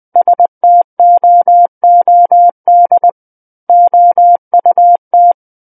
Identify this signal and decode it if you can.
Morse code, heard as a keyed tone, about 20 wpm, 700 Hz. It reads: STOOD OUT
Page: SKCC Learning Center - Morse Code Practice Files